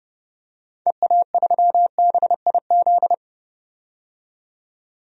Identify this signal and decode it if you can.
Morse code, heard as a keyed tone, about 30 wpm, 700 Hz. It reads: EA3BIZ
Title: DXCC-100 Call Signs - 6 Characters - 30wpm